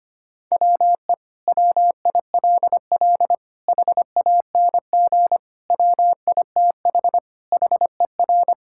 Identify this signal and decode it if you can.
Morse code, heard as a keyed tone, about 25 wpm, 700 Hz. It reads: WE WILL HANG WITH HER
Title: Sentences from Top 1600 Words 25wpm